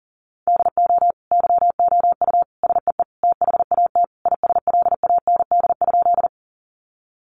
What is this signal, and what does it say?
Morse code, heard as a keyed tone, about 40 wpm, 700 Hz. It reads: DO YOU SEE THAT ISLAND?
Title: Sentences from Top 500 Words 40wpm